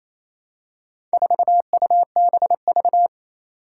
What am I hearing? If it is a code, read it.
Morse code, heard as a keyed tone, about 28 wpm, 700 Hz. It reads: 4UBV
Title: Four Challenging Letter-Number Combinations 28wpm